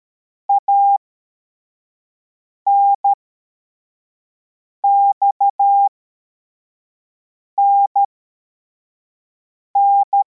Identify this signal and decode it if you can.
Morse code, heard as a keyed tone, about 13 wpm, spaced as for 4 wpm, 800 Hz. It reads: ANXNN
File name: formation / cw / Text23